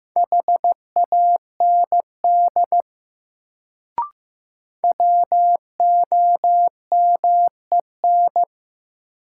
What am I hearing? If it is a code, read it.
Morse code, heard as a keyed tone, about 15 wpm, 700 Hz. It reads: HAND WOMEN
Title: Sets of 3 Words - New Word Drill - 15wpm